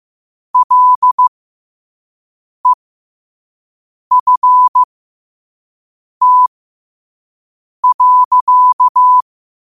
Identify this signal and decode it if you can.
Morse code, heard as a keyed tone, about 15 wpm, spaced as for 5 wpm, 1000 Hz. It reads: LEFT.
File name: qotdfiles / 5wpm